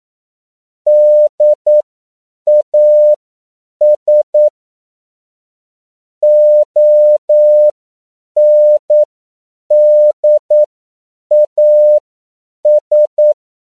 Morse code, heard as a keyed tone, about 9 wpm, spaced as for 7 wpm, 600 Hz. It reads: DAS ONDAS
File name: CW / 10PPM